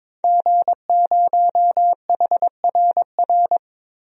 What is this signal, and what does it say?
Morse code, heard as a keyed tone, about 21 wpm, 700 Hz. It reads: G0HRR
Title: DXCC-100 Call Signs - 5 Characters - Repeated 3x - 22wpm